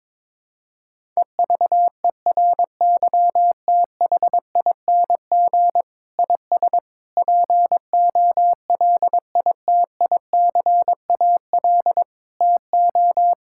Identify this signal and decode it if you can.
Morse code, heard as a keyed tone, about 22 wpm, 700 Hz. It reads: EVERYTHING IS POLITICAL TO
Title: Sentences from Top 1000 Words - Review Part 5 - 22wpm